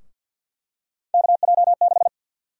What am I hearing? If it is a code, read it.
Morse code, heard as a keyed tone, about 51 wpm, 700 Hz. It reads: KJ6
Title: USA Call Sign Prefixes - 50wpm